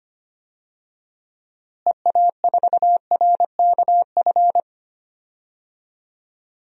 Morse code, heard as a keyed tone, about 25 wpm, 700 Hz. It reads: EA4RKF
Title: DXCC-100 Call Signs - 6 Characters - Repeated 3x - 25wpm